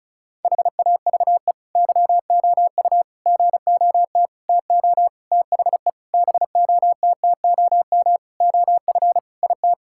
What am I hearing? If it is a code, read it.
Morse code, heard as a keyed tone, about 36 wpm, 700 Hz. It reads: HAVE YOU GOT TO THE BOTTOM OF IT
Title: Sentences from Top 1000 Words - Review Part 6 - 35wpm